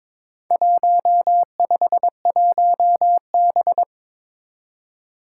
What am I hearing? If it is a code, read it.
Morse code, heard as a keyed tone, about 22 wpm, 700 Hz. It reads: 151B
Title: Four Challenging Letter-Number Combinations 22wpm